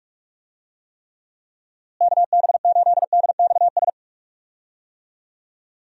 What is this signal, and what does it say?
Morse code, heard as a keyed tone, about 46 wpm, 700 Hz. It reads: KB8DXS
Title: USA Call Signs - 6 Characters - Repeated 2x - 45wpm